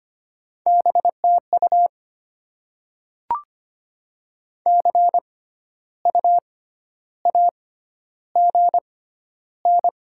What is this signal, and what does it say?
Morse code, heard as a keyed tone, about 25 wpm, spaced as for 12 wpm, 700 Hz. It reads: BTU CU AGN
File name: Top 20 QSO Elements - 6x Character Spacing - 25wpm